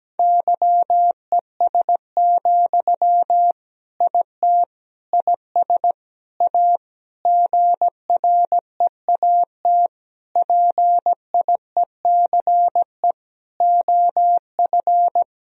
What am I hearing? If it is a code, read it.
Morse code, heard as a keyed tone, about 17 wpm, 700 Hz. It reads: YES, IT IS A GREAT PIECE OF
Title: Sentences from Top 1000 Words - Review Part 2 - 17wpm